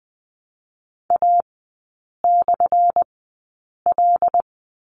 Morse code, heard as a keyed tone, about 20 wpm, spaced as for 8 wpm, 700 Hz. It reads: A/L
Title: 178 Lesson 29e - Slash - Three Characters 20wpm